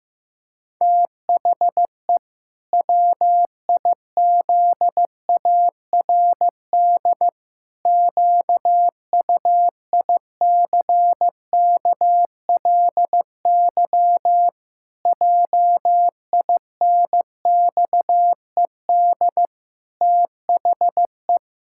Morse code, heard as a keyed tone, about 14 wpm, 700 Hz. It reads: THE WIZARD QUICKLY JINXED THE
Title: Pangram Challenge v2 - 15wpm